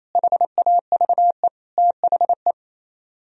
Morse code, heard as a keyed tone, about 36 wpm, 700 Hz. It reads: HAVE THE